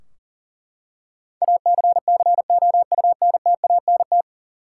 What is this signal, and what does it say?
Morse code, heard as a keyed tone, about 40 wpm, 700 Hz. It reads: ACCOUNTANT
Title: Practice QSO - Exchange 3 - More Info - 40wpm